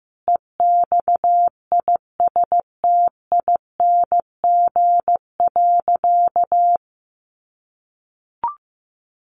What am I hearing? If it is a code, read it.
Morse code, heard as a keyed tone, about 15 wpm, 700 Hz. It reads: EXISTING.
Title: Sentences from Top 1500 Words 15wpm